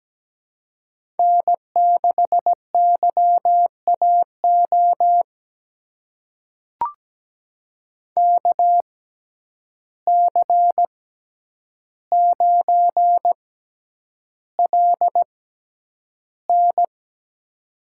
Morse code, heard as a keyed tone, about 17 wpm, spaced as for 9 wpm, 700 Hz. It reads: N6YAO KC9LN
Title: USA Call Signs - 5 Characters - 6x Character Spacing - 17wpm